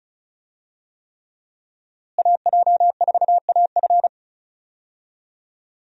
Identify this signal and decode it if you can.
Morse code, heard as a keyed tone, about 36 wpm, 700 Hz. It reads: AJ4AF
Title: USA Call Signs - 5 Characters - Repeated 3x - 35wpm